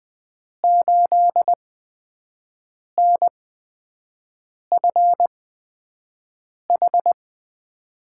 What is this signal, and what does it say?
Morse code, heard as a keyed tone, about 20 wpm, spaced as for 5 wpm, 700 Hz. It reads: 8NFH
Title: USA Call Signs - 6 Characters - 8x Character Spacing - 20wpm